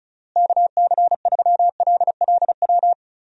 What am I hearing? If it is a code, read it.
Morse code, heard as a keyed tone, about 36 wpm, 700 Hz. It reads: KC3LLW
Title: USA Call Signs - 6 Characters - Repeated 3x - 35wpm